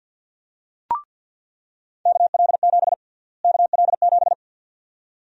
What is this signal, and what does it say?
Morse code, heard as a keyed tone, about 50 wpm, 700 Hz. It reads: KL7 KL7
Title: USA Call Sign Prefixes - Repeated 2x - 50wpm